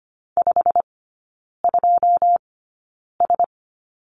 Morse code, heard as a keyed tone, about 25 wpm, spaced as for 9 wpm, 700 Hz. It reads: 52S
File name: Lesson 15e - Number 5 - Three Characters 25wpm